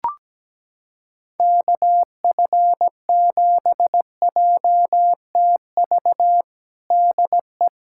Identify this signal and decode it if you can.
Morse code, heard as a keyed tone, about 17 wpm, 700 Hz. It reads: KF7JTV DE
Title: USA Callsign DE Callsign 17wpm